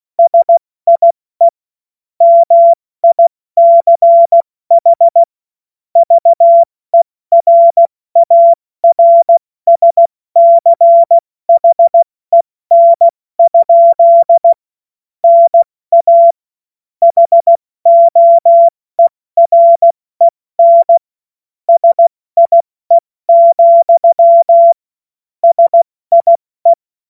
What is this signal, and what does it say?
Morse code, heard as a keyed tone, about 16 wpm, 680 Hz. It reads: SIE MICH VERARSCHEN? NA HOEREN SIE, SIE
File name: Morsekurs / Witze / Witz 29. November 2022 70 F80